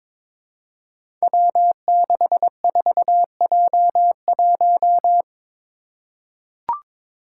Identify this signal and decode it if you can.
Morse code, heard as a keyed tone, about 22 wpm, 700 Hz. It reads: W64J1
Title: Five Challenging Letter-Number Combinations 22wpm